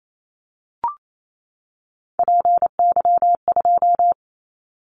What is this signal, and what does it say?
Morse code, heard as a keyed tone, about 28 wpm, 700 Hz. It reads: PY2
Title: Callsign Prefixes for Top 100 DXCC - 28wpm